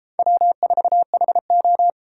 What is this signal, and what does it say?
Morse code, heard as a keyed tone, about 29 wpm, 700 Hz. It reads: W4HO